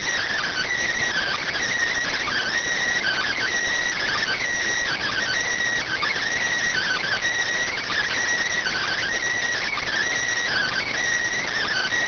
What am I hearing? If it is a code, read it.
Emergency Alert System (SAME) tones (header bursts), heard as harsh buzzing bursts. It is heard from 1.1 to 10.9 s.